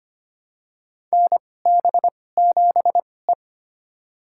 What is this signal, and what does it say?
Morse code, heard as a keyed tone, about 25 wpm, spaced as for 18 wpm, 700 Hz. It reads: NB7E
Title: USA Call Signs - 4 Characters - 2x Character Spacing - 25wpm